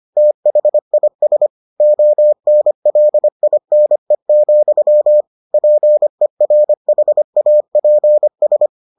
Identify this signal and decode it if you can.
Morse code, heard as a keyed tone, about 25 wpm, 600 Hz. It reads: THIS ONLINE, PERHAPS